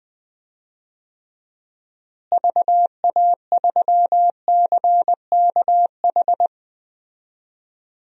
Morse code, heard as a keyed tone, about 20 wpm, 700 Hz. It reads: VA3CKH
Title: DXCC-100 Call Signs - 6 Characters - Repeated 3x - 20wpm